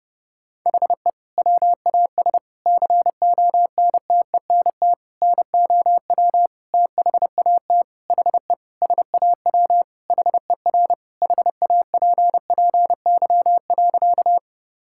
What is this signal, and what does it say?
Morse code, heard as a keyed tone, about 32 wpm, 700 Hz. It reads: HE WAS CONTENT NOW THAT HE SAW HER HAPPY.
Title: Sentences from Top 1000 Words - Review Part 6 - 30wpm